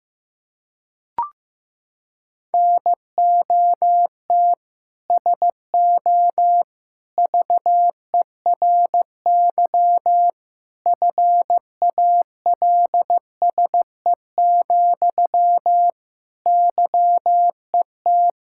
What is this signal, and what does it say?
Morse code, heard as a keyed tone, about 16 wpm, 700 Hz. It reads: NOT SO VERY FALSE, YET
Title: Sentences from Top 1600 Words 15wpm